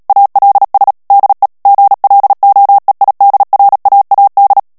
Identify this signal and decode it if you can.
Morse code, heard as a keyed tone, about 37 wpm, 800 Hz. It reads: ALS DE GLOEIDRAAD
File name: morse mp3 / HOLLANDAIS texte clair 2 vitesse 35 mots minute